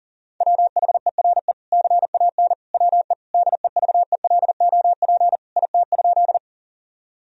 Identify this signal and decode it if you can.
Morse code, heard as a keyed tone, about 40 wpm, 700 Hz. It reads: WHERE CAN WE DEVELOP IT?
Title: Sentences from Top 1000 Words - Review Part 2 - 40wpm